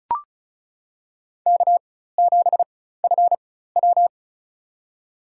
Morse code, heard as a keyed tone, about 34 wpm, spaced as for 16 wpm, 700 Hz. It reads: K7FW
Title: USA Call Signs - 4 Characters - 4x Character Spacing - 35wpm